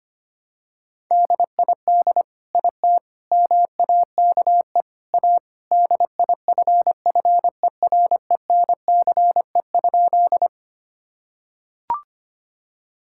Morse code, heard as a keyed tone, about 25 wpm, 700 Hz. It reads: DID IT MAKE A DIFFERENCE?
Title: Sentences from Top 1000 Words - Review Part 5 - 25wpm